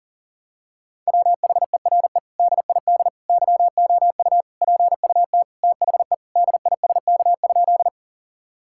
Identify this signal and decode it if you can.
Morse code, heard as a keyed tone, about 40 wpm, 700 Hz. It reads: WHERE DID YOU PUT THE DISK?
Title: Sentences from Top 1000 Words - Review Part 6 - 40wpm